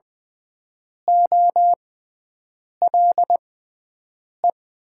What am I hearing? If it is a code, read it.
Morse code, heard as a keyed tone, about 20 wpm, spaced as for 7 wpm, 700 Hz. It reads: OLE
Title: Top 90 QSO Elements - 6x Character Spacing - 20wpm